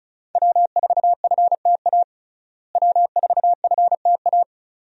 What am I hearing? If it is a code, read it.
Morse code, heard as a keyed tone, about 34 wpm, 700 Hz. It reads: W4FTA W4FTA
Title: USA Call Signs - 5 Characters - Repeated 2x - 35wpm